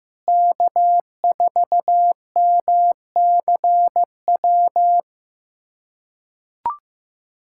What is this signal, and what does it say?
Morse code, heard as a keyed tone, about 15 wpm, 700 Hz. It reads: K4MCW
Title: USA Callsign DE Callsign - 5 Characters - 15wpm